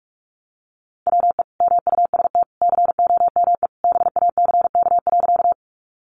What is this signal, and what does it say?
Morse code, heard as a keyed tone, about 44 wpm, 700 Hz. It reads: WE MUST COME BACK.
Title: Sentences from Top 200 Words 45wpm